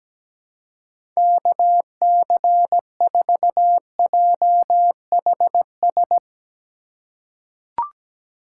Morse code, heard as a keyed tone, about 17 wpm, 700 Hz. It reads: KC4JHS
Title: USA Call Signs - 6 Characters - 2x Character Spacing - 17wpm